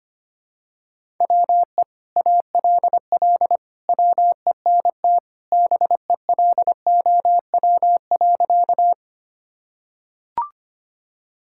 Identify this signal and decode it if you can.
Morse code, heard as a keyed tone, about 25 wpm, 700 Hz. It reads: WE ALL WENT BELOW.
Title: Sentences from Top 1000 Words - Review Part 5 - 25wpm